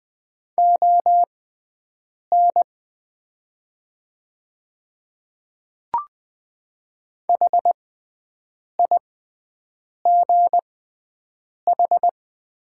Morse code, heard as a keyed tone, about 20 wpm, spaced as for 7 wpm, 700 Hz. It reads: ON HIGH
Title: 6x Character Spacing - Top 300 Words - 20wpm